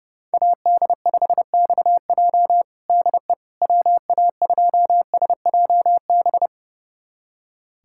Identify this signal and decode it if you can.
Morse code, heard as a keyed tone, about 30 wpm, 700 Hz. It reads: AD5XJ DE WA2SJB
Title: USA Callsign DE Callsign 30wpm